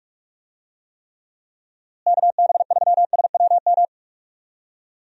Morse code, heard as a keyed tone, about 44 wpm, 700 Hz. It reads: KB3SWM
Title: USA Call Signs - 6 Characters - Repeated 3x - 45wpm